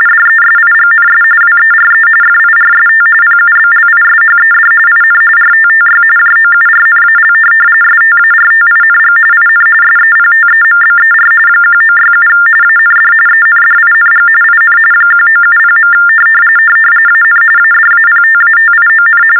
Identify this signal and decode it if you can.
TTY/TDD text telephone tones, about 45.45 baud. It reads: DESIGNS WERE FIRST PUBLISHED IN RTTY JOURNAL STARTING IN SEPTEMBER 1967 AND ENDING IN 1970. AN ADAPTATION
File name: teletype-RTTY-45